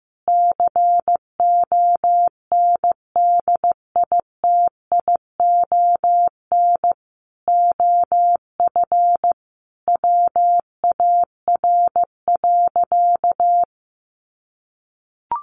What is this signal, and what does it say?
Morse code, heard as a keyed tone, about 15 wpm, 700 Hz. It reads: CONDITION OF WAR.
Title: Sentences from Top 1500 Words 15wpm